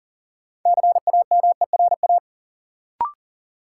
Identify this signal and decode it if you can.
Morse code, heard as a keyed tone, about 40 wpm, 700 Hz. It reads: CAMERA
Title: Sentences from Top 1000 Words - Review Part 5 - New Word Drill - 40wpm